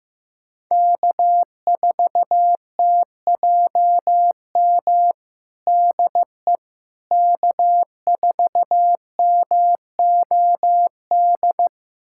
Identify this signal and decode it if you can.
Morse code, heard as a keyed tone, about 15 wpm, 700 Hz. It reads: K4TJM DE K4MOD
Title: USA Callsign DE Callsign - 5 Characters - 15wpm